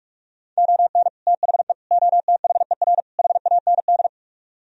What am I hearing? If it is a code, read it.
Morse code, heard as a keyed tone, about 56 wpm, 700 Hz. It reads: ON THE OTHER HAND
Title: Speed-Racing - Sets of 4 Words 45wpm